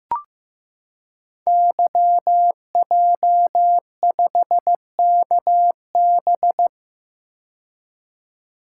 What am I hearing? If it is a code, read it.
Morse code, heard as a keyed tone, about 15 wpm, 700 Hz. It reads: YJ5KB
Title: Code Groups - 5 Letter:Numbers - 15wpm